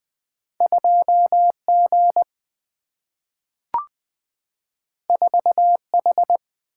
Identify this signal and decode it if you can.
Morse code, heard as a keyed tone, about 20 wpm, 700 Hz. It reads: 2 G 4 H
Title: Single-Letter Single-Number 20wpm